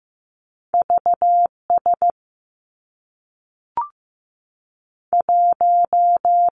Morse code, heard as a keyed tone, about 15 wpm, 700 Hz. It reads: V S 1
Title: Two Challenging Letter-Number Combinations 15wpm